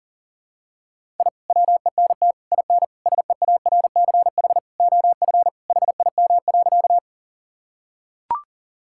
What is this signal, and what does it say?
Morse code, heard as a keyed tone, about 40 wpm, 700 Hz. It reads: I WENT IN SEARCH OF HIM.
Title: Sentences from Top 1000 Words - Review Part 5 - 40wpm